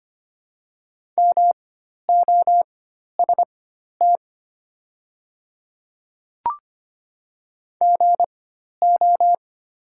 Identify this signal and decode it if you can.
Morse code, heard as a keyed tone, about 25 wpm, spaced as for 12 wpm, 700 Hz. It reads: MOST GO
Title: 4x Character Spacing - Top 100 Words - 25wpm